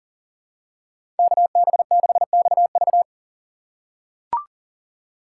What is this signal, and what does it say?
Morse code, heard as a keyed tone, about 40 wpm, 700 Hz. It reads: KB6XV
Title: USA Call Signs - 5 Characters - 2x Character Spacing - 40wpm